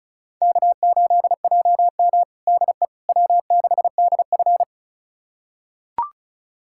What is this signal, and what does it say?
Morse code, heard as a keyed tone, about 36 wpm, 700 Hz. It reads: K8JM DE W6DF
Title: USA Callsign DE Callsign - 4 Characters - 35wpm